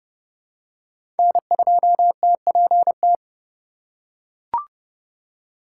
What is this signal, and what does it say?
Morse code, heard as a keyed tone, about 30 wpm, 700 Hz. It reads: N2TPT
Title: USA Call Signs - 5 Characters - 2x Character Spacing - 30wpm